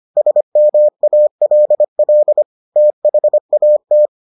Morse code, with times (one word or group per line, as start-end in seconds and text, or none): 0.2-2.4 SMALL
2.8-4.1 THAT